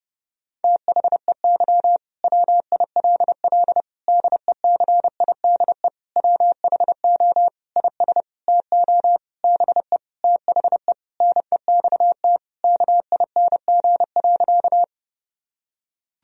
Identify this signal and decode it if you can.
Morse code, heard as a keyed tone, about 30 wpm, 700 Hz. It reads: THEY WILL DECIDE WHO IS TO BE THE NEXT KING.
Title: Sentences from Top 1000 Words - Review Part 3 - 30wpm